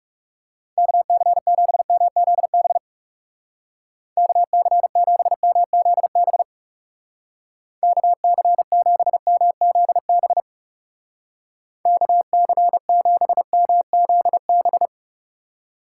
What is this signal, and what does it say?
Morse code, heard as a keyed tone, about 37 wpm, 700 Hz. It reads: KC7MZB KC7MZB KC7MZB KC7MZB
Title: Speed Racing - USA 2x3 Call Signs 30wpm